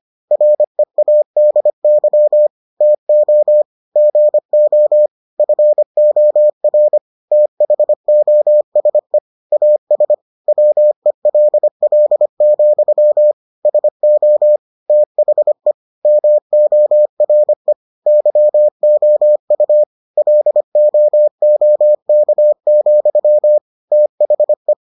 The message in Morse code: READY TO GO FOR THOSE AS WELL, SO THE MORE YOU LOOK, THE

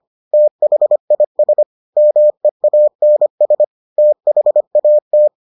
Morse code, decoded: THIS MEANS THAT

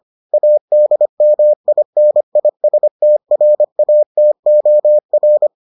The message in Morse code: ADMINISTRATOR

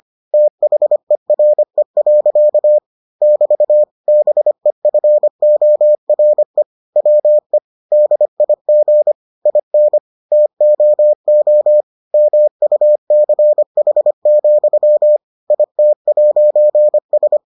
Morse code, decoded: THERE. <BT> BEFORE WE DIG IN TOO MUCH, IT'S

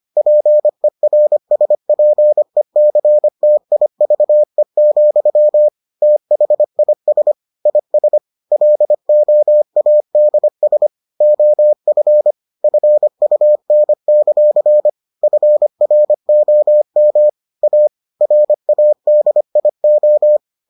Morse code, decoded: PERSPECTIVE, THIS IS LOADS OF FUN; FROM A RADIO

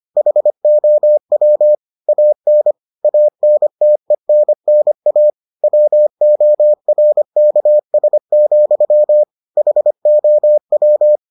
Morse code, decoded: HOW AN ANTENNA WORKS, HOW

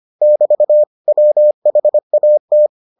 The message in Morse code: <BT> WHAT

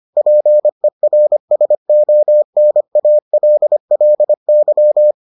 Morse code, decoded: PERSONALLY

